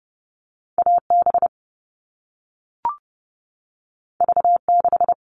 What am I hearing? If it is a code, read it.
Morse code, heard as a keyed tone, about 30 wpm, 700 Hz. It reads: AB V6
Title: All Cut Numbers - 2-Digit 30wpm